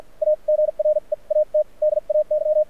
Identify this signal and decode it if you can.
Morse code, heard as a keyed tone, about 52 wpm, spaced as for 41 wpm, 590 Hz. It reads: A GREAT DAY
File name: archived / CRSnet